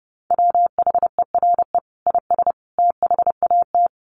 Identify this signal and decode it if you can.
Morse code, heard as a keyed tone, about 30 wpm, 700 Hz. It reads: WHERE IS THAT
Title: Sentences from Top 1300 Words 30wpm